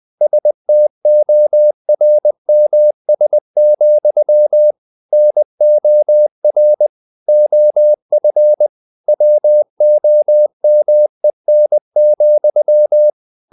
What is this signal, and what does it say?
Morse code, heard as a keyed tone, about 20 wpm, 600 Hz. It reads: STORMS, NOR OF WOMEN,
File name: the old man and the sea / the old man and the sea part 1 20wpm